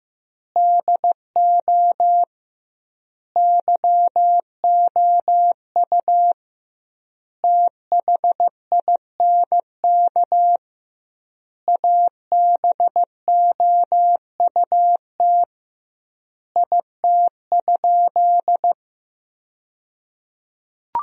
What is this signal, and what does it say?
Morse code, heard as a keyed tone, about 15 wpm, 700 Hz. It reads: DO YOU THINK ABOUT IT?
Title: 2x Word-Spacing - Sentences from Top 200 Words - 15wpm